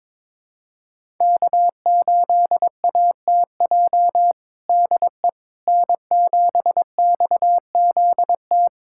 Morse code, heard as a keyed tone, about 21 wpm, 700 Hz. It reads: K8ATJ DE N7XZT
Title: USA Callsign DE Callsign - 5 Characters - 22wpm